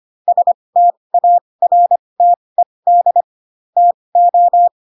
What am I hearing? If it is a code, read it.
Morse code, heard as a keyed tone, about 25 wpm, spaced as for 20 wpm, 700 Hz. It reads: STARTED TO
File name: Pinocchio / Pinocchio ch7 15 25 ik1hge